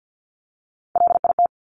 Morse code, heard as a keyed tone, about 50 wpm, 700 Hz. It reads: LIT
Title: Top Three Letter Combinations in Words 50wpm